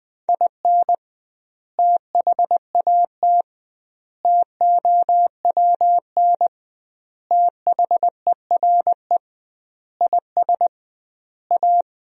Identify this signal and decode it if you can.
Morse code, heard as a keyed tone, about 20 wpm, 700 Hz. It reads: IN THAT TOWN THERE IS A
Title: 2x Word-Spacing - Sentences from Top 500 Words - 20wpm